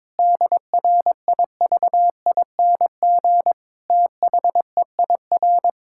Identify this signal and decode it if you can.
Morse code, heard as a keyed tone, about 22 wpm, 700 Hz. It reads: DRIVING THEIR
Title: Pangram Challenge v2 - 22wpm